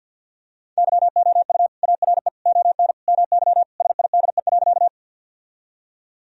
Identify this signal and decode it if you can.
Morse code, heard as a keyed tone, about 51 wpm, 700 Hz. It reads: YOU ARE ON MY SIDE.
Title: Sentences from Top 1000 Words - Review Part 1 - 50wpm